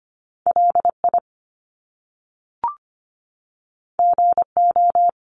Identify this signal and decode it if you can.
Morse code, heard as a keyed tone, about 25 wpm, 700 Hz. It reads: LI GO
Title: Top Two Letter Combinations in Words 25wpm